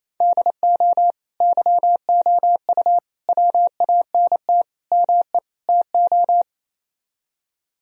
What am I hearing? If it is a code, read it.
Morse code, heard as a keyed tone, about 34 wpm, 700 Hz. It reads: DO YOU WANT ME TO
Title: Speed-Racing - Sets of 5 Words 28wpm